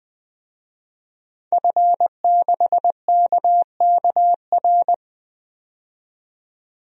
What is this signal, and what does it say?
Morse code, heard as a keyed tone, about 20 wpm, 700 Hz. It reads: F6KKR
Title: DXCC-100 Call Signs - 5 Characters - 20wpm